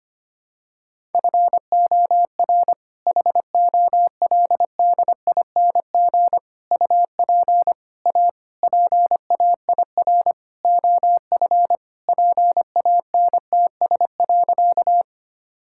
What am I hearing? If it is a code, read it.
Morse code, heard as a keyed tone, about 25 wpm, 700 Hz. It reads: FOR HOLDING UP A PAIR OF PANTS.
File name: practice / 25wpm / All-Age Appropriate Jokes - 25wpm